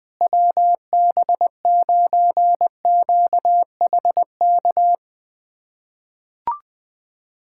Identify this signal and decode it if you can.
Morse code, heard as a keyed tone, about 20 wpm, 700 Hz. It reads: WB9QHK
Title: USA Callsign DE Callsign 20wpm